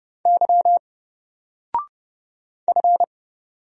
Morse code, heard as a keyed tone, about 30 wpm, 700 Hz. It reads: Y F
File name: Single-Letter-Number-30wpm